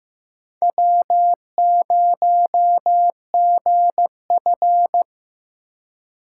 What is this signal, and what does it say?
Morse code, heard as a keyed tone, about 15 wpm, 700 Hz. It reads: W0GF
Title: USA Call Signs - 4 Characters - 4x Character Spacing - 15wpm